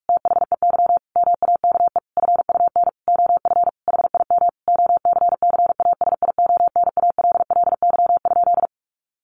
Morse code, heard as a keyed tone, about 44 wpm, 700 Hz. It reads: THEY MAKE FUN OF HIM OCCASIONALLY?
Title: Sentences from Top 1300 Words 45wpm